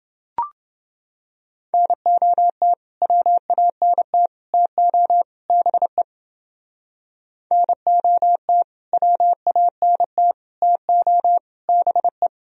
Morse code, heard as a keyed tone, about 30 wpm, 700 Hz. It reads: NOT WANT TO BE NOT WANT TO BE